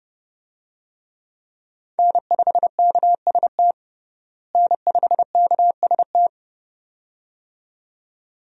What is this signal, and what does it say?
Morse code, heard as a keyed tone, about 30 wpm, 700 Hz. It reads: N5KST N5KST
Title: USA Call Signs - 5 Characters - Repeated 2x - 30wpm